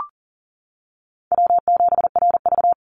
Morse code, heard as a keyed tone, about 39 wpm, 700 Hz. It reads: W7RV
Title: NA SOTA Chaser Callsigns 40wpm